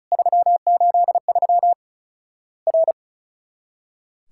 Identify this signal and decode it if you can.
Morse code, heard as a keyed tone, about 36 wpm, 700 Hz. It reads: 383 R
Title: Worked All Europe DX Contest - QTC Exchange - 35wpm